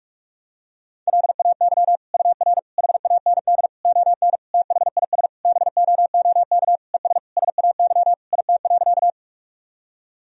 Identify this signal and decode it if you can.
Morse code, heard as a keyed tone, about 46 wpm, 700 Hz. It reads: LAY UR HAND ON THIS BOOK ES SAY IT.
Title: Sentences from Top 400 Words and Abbreviations 45wpm